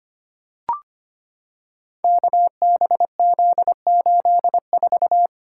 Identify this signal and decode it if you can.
Morse code, heard as a keyed tone, about 25 wpm, 700 Hz. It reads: KBZ84
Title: Code Groups - 5 Letter:Numbers - 25wpm